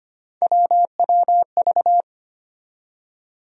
Morse code, heard as a keyed tone, about 25 wpm, 700 Hz. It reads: WWV
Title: USA Call Sign Postfixes - 3 Letters 25wpm